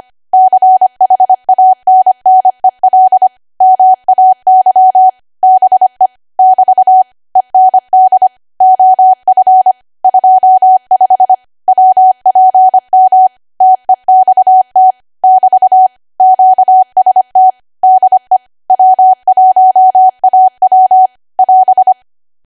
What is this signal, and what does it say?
Morse code, heard as a keyed tone, about 25 wpm, 750 Hz. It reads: CHANNEL MAY BE <BT> END OF 25 WPM TEXT <BT> QST DE W1AW <AS>
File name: Archive / 25 WPM / 25WPM